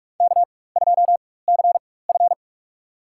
Morse code, heard as a keyed tone, about 46 wpm, spaced as for 21 wpm, 700 Hz. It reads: K2CF
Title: USA Call Signs - 4 Characters - 4x Character Spacing - 45wpm